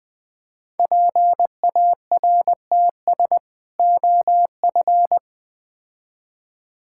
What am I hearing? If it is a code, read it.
Morse code, heard as a keyed tone, about 20 wpm, 700 Hz. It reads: PARTS OF
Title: Build A Phrase - 3-Word Phrases - 20wpm